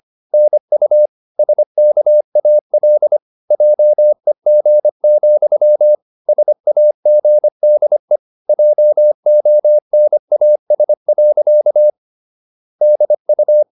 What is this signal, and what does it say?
Morse code, heard as a keyed tone, about 25 wpm, 600 Hz. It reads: NU SKAL JEG, SAGDE JONAS. DU
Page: Ferie 25 wpm